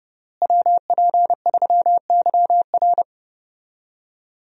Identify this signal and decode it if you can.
Morse code, heard as a keyed tone, about 30 wpm, 700 Hz. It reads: WP3YR
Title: USA Call Signs - 5 Characters - Repeated 2x - 30wpm